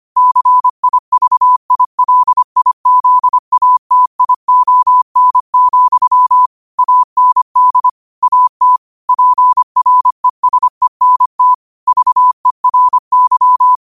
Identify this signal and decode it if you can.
Morse code, heard as a keyed tone, about 25 wpm, 1000 Hz. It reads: CIVILIZATION, AND AT PRESENT VERY
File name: qotdfiles / 25wpm